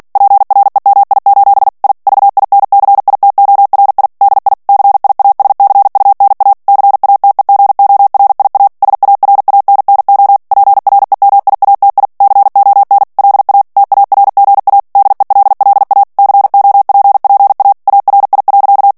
Morse code, heard as a keyed tone, about 48 wpm, 800 Hz. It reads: PREMI: I VINCITORI DI CIASCUNA CATEGORIA SARANNO PREMIATI CON LA TARGA DELLA COPPA ARI.
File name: ITALIEN texte clair 2 vitesse 45 mots minute